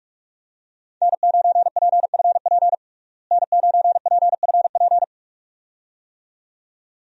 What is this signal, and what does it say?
Morse code, heard as a keyed tone, about 46 wpm, 700 Hz. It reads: N9PFP N9PFP
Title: USA Call Signs - 5 Characters - Repeated 2x - 45wpm